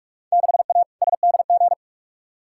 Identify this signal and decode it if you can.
Morse code, heard as a keyed tone, about 46 wpm, 700 Hz. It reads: 6A SDG
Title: Field Day Exchanges Encore - 45wpm